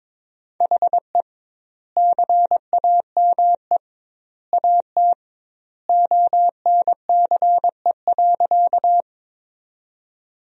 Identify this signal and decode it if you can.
Morse code, heard as a keyed tone, about 22 wpm, 700 Hz. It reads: HE CAME AT ONCE.
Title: 2x Word-Spacing - Sentences from Top 300 Words - 22wpm